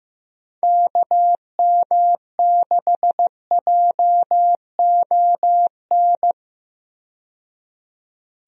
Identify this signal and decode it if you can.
Morse code, heard as a keyed tone, about 15 wpm, 700 Hz. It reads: KM6JON
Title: USA Call Signs - 6 Characters - 15wpm